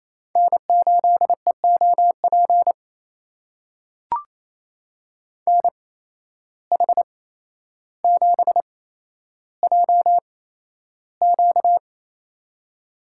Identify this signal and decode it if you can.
Morse code, heard as a keyed tone, about 28 wpm, spaced as for 12 wpm, 700 Hz. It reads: N8EOP NH7JQ
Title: USA Call Signs - 5 Characters - 8x Character Spacing - 28wpm